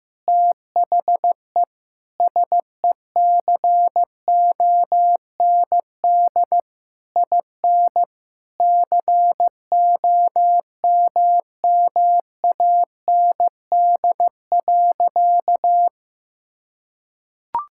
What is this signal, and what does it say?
Morse code, heard as a keyed tone, about 15 wpm, 700 Hz. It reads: THE SECOND IN COMMAND.
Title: Sentences from Top 1600 Words 15wpm